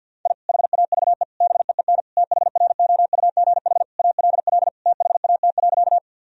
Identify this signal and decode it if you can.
Morse code, heard as a keyed tone, about 51 wpm, 700 Hz. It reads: I HAVE BEEN THROUGH ALL THAT.
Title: Sentences from Top 1000 Words - Review Part 1 - 50wpm